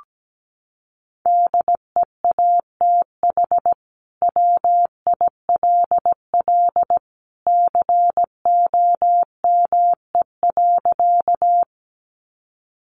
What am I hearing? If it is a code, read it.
Morse code, heard as a keyed tone, about 17 wpm, 700 Hz. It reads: DEATH WILL COME.
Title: Sentences from Top 1100 Words 17wpm